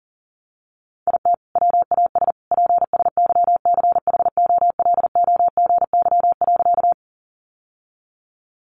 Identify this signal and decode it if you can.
Morse code, heard as a keyed tone, about 40 wpm, 700 Hz. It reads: IT WAS PSYCHOLOGY.
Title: Sentences from Top 1400 Words 40wpm